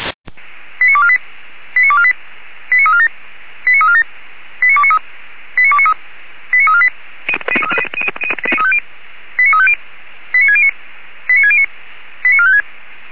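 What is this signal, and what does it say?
Five-tone selective call (ZVEI), heard as quick runs of five tones, tones of about 70 ms. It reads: 89248 89248 89347 89347 89293 89293 89348 89350 89609 89609 89456
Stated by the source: ZVEI five-tone selective call 89288 received by SDR, logged on the Feuerwehr Orlamünde (fire brigade) audio server.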